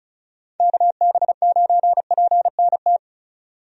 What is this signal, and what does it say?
Morse code, heard as a keyed tone, about 34 wpm, 700 Hz. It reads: KB9PNT
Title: USA Call Signs - 6 Characters - 35wpm